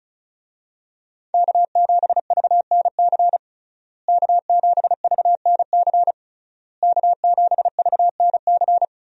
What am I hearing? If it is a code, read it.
Morse code, heard as a keyed tone, about 36 wpm, 700 Hz. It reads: K7VNC K7VNC K7VNC
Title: USA Call Signs - 5 Characters - Repeated 3x - 35wpm